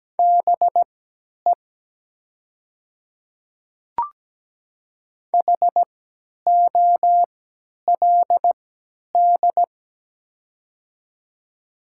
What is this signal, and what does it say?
Morse code, heard as a keyed tone, about 17 wpm, spaced as for 10 wpm, 700 Hz. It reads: BE HOLD
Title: 3x Character Spacing - Top 500 Words - 17wpm